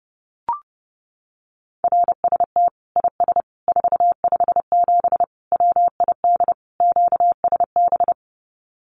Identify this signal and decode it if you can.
Morse code, heard as a keyed tone, about 29 wpm, 700 Hz. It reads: RST IS 457 WID QSB
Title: Common QSO Phrases 30wpm